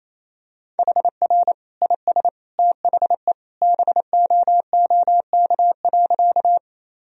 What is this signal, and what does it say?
Morse code, heard as a keyed tone, about 28 wpm, 700 Hz. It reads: HR IS THE BOOK.
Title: Sentences from Top 300 Words and Abbreviations 28wpm